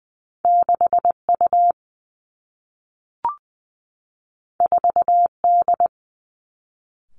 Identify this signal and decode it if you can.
Morse code, heard as a keyed tone, about 20 wpm, 700 Hz. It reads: 6 U 4 D
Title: Two Challenging Letter-Number Combinations 20wpm